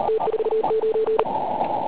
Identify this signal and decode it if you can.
Morse code, heard as a keyed tone, about 38 wpm, 440 Hz, with noise. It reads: T49
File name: Pile UP CUC204